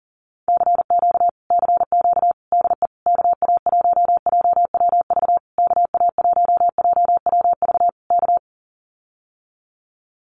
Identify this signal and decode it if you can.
Morse code, heard as a keyed tone, about 41 wpm, 700 Hz. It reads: CQ CQ DE KA1JWV KA1JWV K
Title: Common QSO Phrases Encore 40wpm